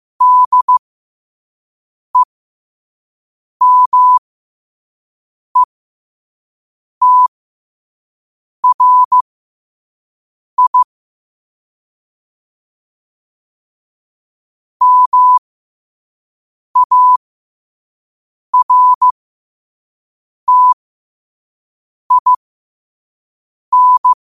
Morse code, heard as a keyed tone, about 15 wpm, spaced as for 5 wpm, 1000 Hz. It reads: DEMETRI MARTIN
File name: qotdfiles / 5wpm